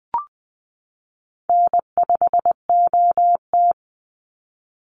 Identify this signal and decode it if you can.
Morse code, heard as a keyed tone, about 19 wpm, 700 Hz. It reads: N5OT
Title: Call Signs Repeated 3x - 20wpm